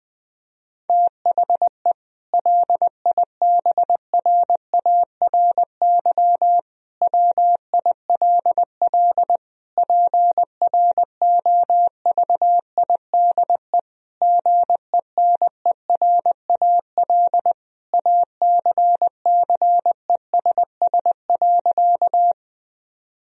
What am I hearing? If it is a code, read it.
Morse code, heard as a keyed tone, about 20 wpm, 700 Hz. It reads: THE LIBRARY WILL PROVIDE GENERAL ACCESS.
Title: Sentences from Top 1000 Words - Review Part 4 - 20wpm